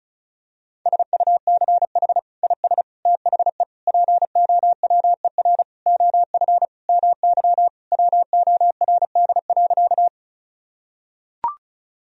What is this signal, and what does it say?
Morse code, heard as a keyed tone, about 36 wpm, 700 Hz. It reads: SUCH IS THE POWER OF MY WORD.
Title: Sentences from Top 1000 Words - Review Part 2 - 35wpm